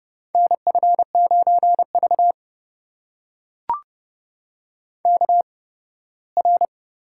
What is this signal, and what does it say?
Morse code, heard as a keyed tone, about 30 wpm, spaced as for 18 wpm, 700 Hz. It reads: NF9V KR
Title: USA Call Signs - 8x Character Spacing - 30wpm